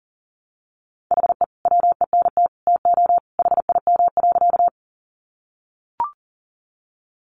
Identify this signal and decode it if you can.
Morse code, heard as a keyed tone, about 40 wpm, 700 Hz. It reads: HE WENT TO HIM.
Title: Sentences from Top 200 Words 40wpm